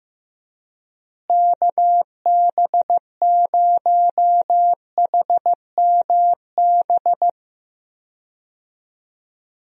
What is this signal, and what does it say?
Morse code, heard as a keyed tone, about 15 wpm, 700 Hz. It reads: KB0HMB
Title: USA Call Signs - 6 Characters - Repeated 2x - 15wpm